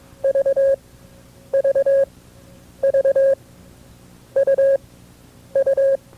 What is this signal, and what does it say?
Morse code, heard as a keyed tone, about 22 wpm, spaced as for 8 wpm, 540 Hz. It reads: VVVUU
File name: FreeFiles / CW-practice / KB6MT-tape2-side2